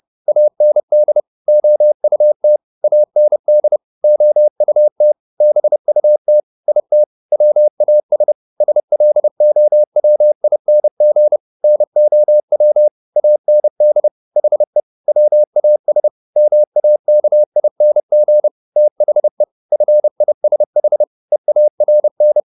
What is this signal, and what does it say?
Morse code, heard as a keyed tone, about 30 wpm, 600 Hz. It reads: AND OUT AND OUT BUT IT WAS SLOWING NOW AND HE WAS MAKING THE FISH EARN